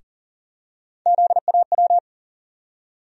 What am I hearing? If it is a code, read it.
Morse code, heard as a keyed tone, about 40 wpm, 700 Hz. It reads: ZAW
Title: USA Call Sign Postfixes - 3 Letters 40wpm